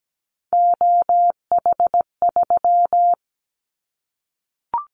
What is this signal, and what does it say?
Morse code, heard as a keyed tone, about 17 wpm, 700 Hz. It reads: OH3
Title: Callsign Prefixes for Top 25 DXCC - Repeated 3x - 17wpm